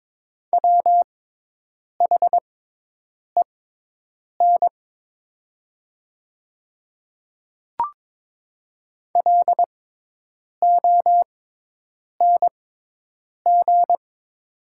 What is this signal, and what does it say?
Morse code, heard as a keyed tone, about 22 wpm, spaced as for 8 wpm, 700 Hz. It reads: WHEN LONG
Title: 6x Character Spacing - Top 100 Words - 22wpm